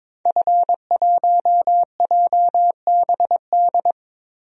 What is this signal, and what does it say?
Morse code, heard as a keyed tone, about 22 wpm, 700 Hz. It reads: F1JBD
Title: DXCC-100 Call Signs - 5 Characters - Repeated 3x - 22wpm